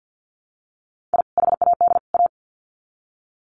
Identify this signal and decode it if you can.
Morse code, heard as a keyed tone, about 50 wpm, 700 Hz. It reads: I HAD A
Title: Sets of 3 Words 50wpm